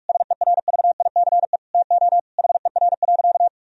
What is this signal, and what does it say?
Morse code, heard as a keyed tone, about 46 wpm, 700 Hz. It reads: SERVICE TO HER.
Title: Sentences from Top 1000 Words - Review Part 4 - 45wpm